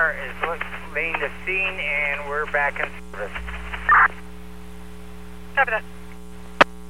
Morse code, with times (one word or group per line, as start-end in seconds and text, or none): none